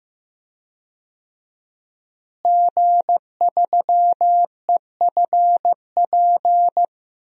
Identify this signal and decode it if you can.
Morse code, heard as a keyed tone, about 15 wpm, 700 Hz. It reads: G3EFP
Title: DXCC-100 Call Signs - 5 Characters - Repeated 3x - 15wpm